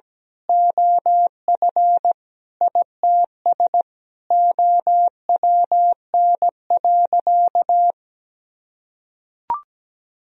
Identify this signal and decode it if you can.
Morse code, heard as a keyed tone, about 17 wpm, 700 Hz. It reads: OF ITS OWN.
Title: Sentences from Top 1000 Words - Review Part 4 - 17wpm